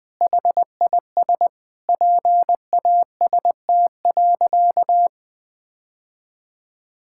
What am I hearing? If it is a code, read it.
Morse code, heard as a keyed tone, about 21 wpm, 700 Hz. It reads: HIS PAST.
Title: Sentences with 5-Word Phrases - 20wpm